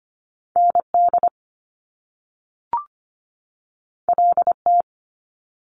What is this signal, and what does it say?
Morse code, heard as a keyed tone, about 25 wpm, 700 Hz. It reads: ND LT
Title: Top Two Letter Combinations in Words 25wpm